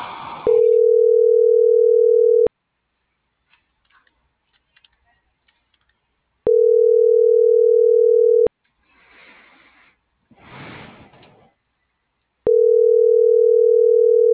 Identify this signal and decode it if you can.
Ringback tone (North America), 440 + 480 Hz.
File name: CENTRAL MAERSK / out isabel.bravo - .86274